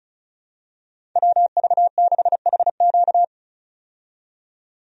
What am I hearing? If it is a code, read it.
Morse code, heard as a keyed tone, about 36 wpm, 700 Hz. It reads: WV6HQ
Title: USA Call Signs - 5 Characters - 35wpm